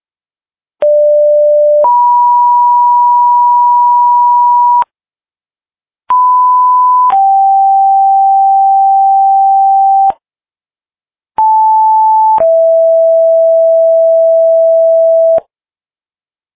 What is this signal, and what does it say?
Two-tone page, heard as a one-second tone, then a three-second tone, about 590 then 970 Hz, about 1000 then 780 Hz, about 880 then 660 Hz. It is heard from 0.8 to 15.4 s.